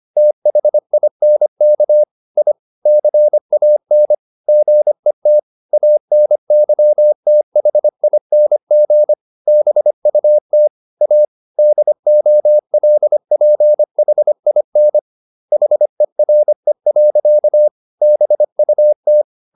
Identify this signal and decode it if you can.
Morse code, heard as a keyed tone, about 25 wpm, 600 Hz. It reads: THINK I CAN GET ANYTHING BUT A DOLPHIN HERE. BUT
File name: the old man and the sea part 1 25wpm